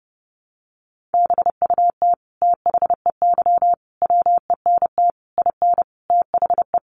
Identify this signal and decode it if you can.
Morse code, heard as a keyed tone, about 30 wpm, 700 Hz. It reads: BUT THEY WENT IN THE
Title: Sentences from Top 1300 Words 30wpm